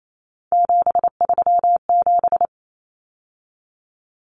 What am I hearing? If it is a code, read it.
Morse code, heard as a keyed tone, about 28 wpm, 700 Hz. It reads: 737
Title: Numbers - 3-Digit 28wpm